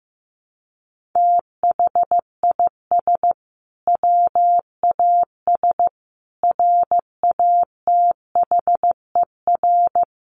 Morse code, heard as a keyed tone, about 15 wpm, 700 Hz. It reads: THIS WAS RATHER
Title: Sentences from Top 1400 Words 15wpm